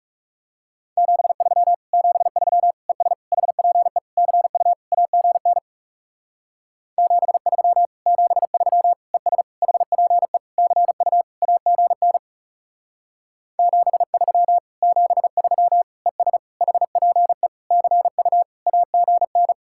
Morse code, heard as a keyed tone, about 39 wpm, 700 Hz. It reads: 73 73 ES HPE CU AGN 73 73 ES HPE CU AGN 73 73 ES HPE CU AGN